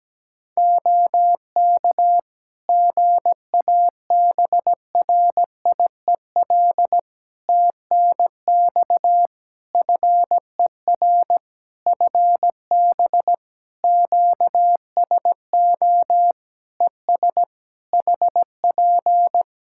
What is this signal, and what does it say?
Morse code, heard as a keyed tone, about 17 wpm, 700 Hz. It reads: OK GABRIEL TNX FER FB QSO ES HP
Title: Practice QSO - Exchange 4 - Ending - 17wpm